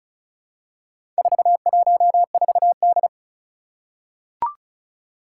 Morse code, heard as a keyed tone, about 36 wpm, 700 Hz. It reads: 414D
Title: Four Challenging Letter-Number Combinations 35wpm